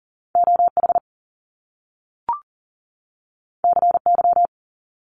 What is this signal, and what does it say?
Morse code, heard as a keyed tone, about 39 wpm, 700 Hz. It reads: OH CY
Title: Top Two Letter Combinations in Words 40wpm